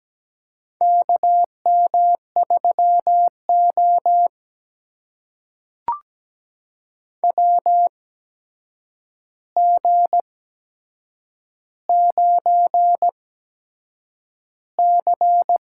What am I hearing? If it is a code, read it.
Morse code, heard as a keyed tone, about 17 wpm, spaced as for 7 wpm, 700 Hz. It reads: KM3O WG9C
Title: USA Call Signs - 4 Characters - 8x Character Spacing - 17wpm